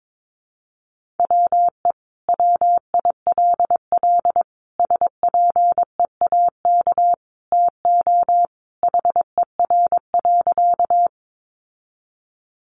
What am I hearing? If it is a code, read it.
Morse code, heard as a keyed tone, about 22 wpm, 700 Hz. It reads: WE WILL SPEAK TO HER.
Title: Sentences from Top 1100 Words 22wpm